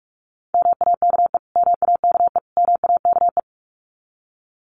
Morse code, heard as a keyed tone, about 44 wpm, 700 Hz. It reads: MAKE MAKE MAKE
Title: Top 200 Words Repeated 45wpm